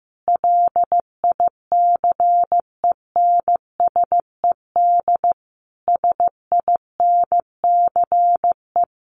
Morse code, heard as a keyed tone, about 15 wpm, 700 Hz. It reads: LICENSED SINCE
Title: Common QSO Phrases 15wpm